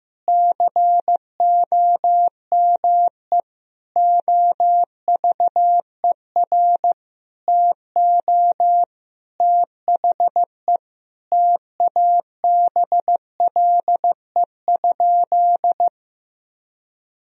Morse code, heard as a keyed tone, about 16 wpm, 700 Hz. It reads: COME OVER TO THE TABLE?
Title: Sentences from Top 1000 Words - Review Part 2 - 15wpm